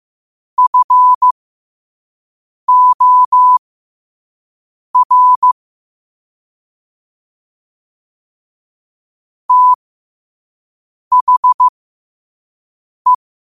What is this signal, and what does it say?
Morse code, heard as a keyed tone, about 15 wpm, spaced as for 5 wpm, 1000 Hz. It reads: FOR THE
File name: qotdfiles / 5wpm